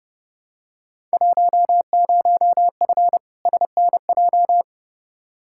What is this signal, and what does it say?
Morse code, heard as a keyed tone, about 30 wpm, 700 Hz. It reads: 10F SNJ
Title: Field Day Exchanges Encore - 30wpm